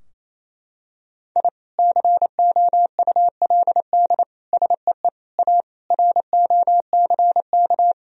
Morse code, heard as a keyed tone, about 28 wpm, 700 Hz. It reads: I COULD SEE A ROCK
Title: Sentences from Top 1000 Words - Review Part 2 - 28wpm